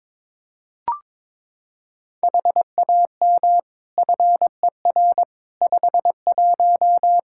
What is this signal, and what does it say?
Morse code, heard as a keyed tone, about 21 wpm, 700 Hz. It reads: HAM FER 51
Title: Practice QSO - Exchange 3 - More Info - 22wpm